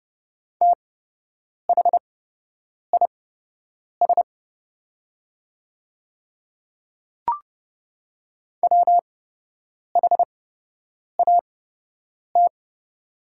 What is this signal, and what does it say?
Morse code, heard as a keyed tone, about 30 wpm, spaced as for 8 wpm, 700 Hz. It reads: THIS WHAT
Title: 8x Character Spacing - Top 100 Words - 30wpm